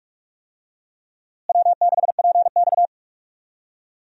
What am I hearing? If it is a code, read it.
Morse code, heard as a keyed tone, about 46 wpm, 700 Hz. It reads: W6PX
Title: USA Call Signs - 4 Characters - Repeated 3x - 45wpm